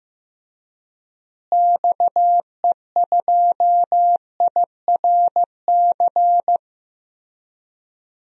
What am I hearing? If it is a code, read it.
Morse code, heard as a keyed tone, about 15 wpm, 700 Hz. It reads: XE2IRC
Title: DXCC-100 Call Signs - 6 Characters - Repeated 3x - 15wpm